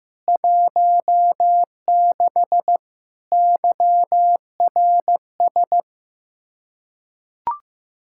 Morse code, heard as a keyed tone, about 15 wpm, 700 Hz. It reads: 16 YRS
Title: Practice QSO - Exchange 3 - More Info - 15wpm